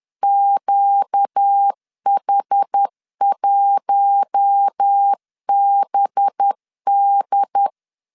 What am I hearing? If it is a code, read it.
Morse code, heard as a keyed tone, about 10 wpm, 800 Hz. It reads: QH1BD
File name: CW / DET / 1 6 1500C S B 10wpm 4137